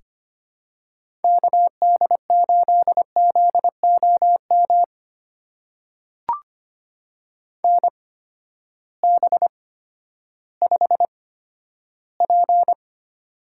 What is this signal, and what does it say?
Morse code, heard as a keyed tone, about 25 wpm, spaced as for 12 wpm, 700 Hz. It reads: KD8ZOM NB5P
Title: USA Call Signs - 8x Character Spacing - 25wpm